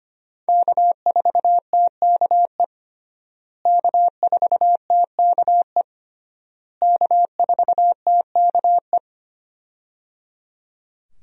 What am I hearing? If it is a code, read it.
Morse code, heard as a keyed tone, about 25 wpm, 700 Hz. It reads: K4TKE K4TKE K4TKE
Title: USA Call Signs - 5 Characters - Repeated 3x - 25wpm